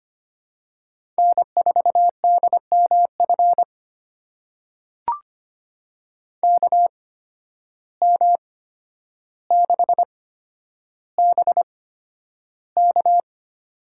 Morse code, heard as a keyed tone, about 25 wpm, spaced as for 11 wpm, 700 Hz. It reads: N4DMF KM6BK
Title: USA Call Signs - 5 Characters - 8x Character Spacing - 25wpm